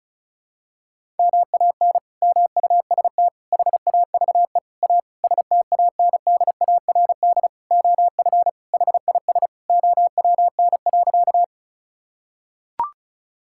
Morse code, heard as a keyed tone, about 36 wpm, 700 Hz. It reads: MAN MUST HAVE A STANDARD OF HIS OWN.
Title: Sentences from Top 1000 Words - Review Part 6 - 35wpm